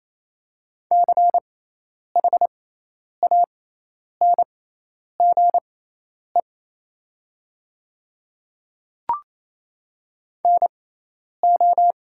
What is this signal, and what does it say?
Morse code, heard as a keyed tone, about 27 wpm, spaced as for 10 wpm, 700 Hz. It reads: CHANGE NO
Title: 6x Character Spacing - Top 200 Words - 28wpm